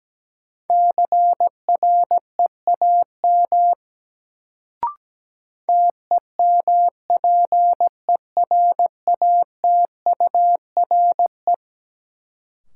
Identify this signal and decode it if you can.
Morse code, heard as a keyed tone, about 17 wpm, 700 Hz. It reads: CREAM TEMPERATURE
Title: Sentences from Top 1500 Words - New Word Drill Alt - 17wpm